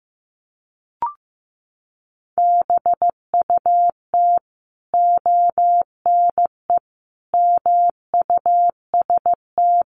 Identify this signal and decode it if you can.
Morse code, heard as a keyed tone, about 15 wpm, 700 Hz. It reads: BUT ONE MUST
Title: Sentences from Top 1100 Words 15wpm